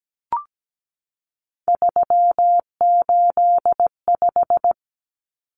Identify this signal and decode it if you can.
Morse code, heard as a keyed tone, about 17 wpm, 700 Hz. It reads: 385
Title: Common Cut Numbers - 3-Digit 17wpm